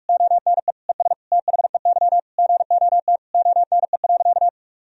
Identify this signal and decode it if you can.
Morse code, heard as a keyed tone, about 41 wpm, 700 Hz. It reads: ONE ES THEY GOT ONE.
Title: Sentences from Top 300 Words and Abbreviations 45wpm